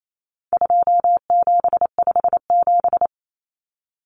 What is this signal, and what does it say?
Morse code, heard as a keyed tone, about 28 wpm, 700 Hz. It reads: 2757
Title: Common Cut Numbers - 4-Digit 28wpm